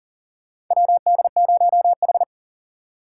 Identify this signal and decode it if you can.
Morse code, heard as a keyed tone, about 40 wpm, 700 Hz. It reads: WD0H
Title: USA Call Signs - 4 Characters - 2x Character Spacing - 40wpm